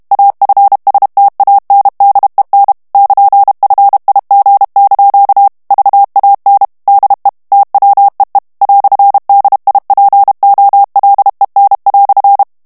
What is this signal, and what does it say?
Morse code, heard as a keyed tone, about 32 wpm, 800 Hz. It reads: AFSTANDEN <KN>FIG) VAN DE TWEE "DIPOLEN"
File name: morse mp3 / HOLLANDAIS texte clair 1 vitesse 30 mots minute